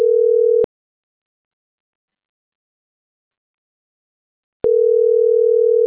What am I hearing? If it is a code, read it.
Ringback tone (North America), 440 + 480 Hz.